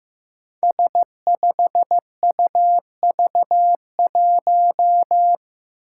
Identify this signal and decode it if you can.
Morse code, heard as a keyed tone, about 15 wpm, 700 Hz. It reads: S5UV1
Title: Five Challenging Letter-Number Combinations 15wpm